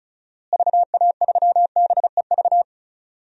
Morse code, heard as a keyed tone, about 36 wpm, 700 Hz. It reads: VA3BEV
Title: DXCC-100 Call Signs - 6 Characters - Repeated 3x - 35wpm